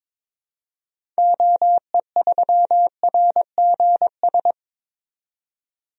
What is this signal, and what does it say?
Morse code, heard as a keyed tone, about 22 wpm, 700 Hz. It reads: OE3RGS
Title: DXCC-100 Call Signs - 6 Characters - 22wpm